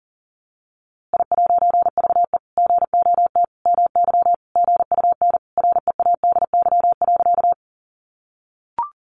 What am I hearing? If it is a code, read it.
Morse code, heard as a keyed tone, about 40 wpm, 700 Hz. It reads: I'VE GOT MY GUN READY.
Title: A Sound Of Thunder 40wpm